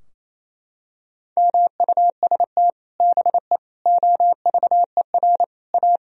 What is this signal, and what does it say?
Morse code, heard as a keyed tone, about 27 wpm, 700 Hz. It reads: MUST BE OVER A
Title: Sentences from Top 1000 Words - Review Part 2 - 28wpm